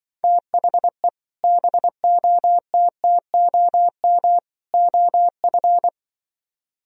Morse code, heard as a keyed tone, about 24 wpm, 700 Hz. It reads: THE BOTTOM OF
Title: Speed-Racing - Sets of 3 Words 20wpm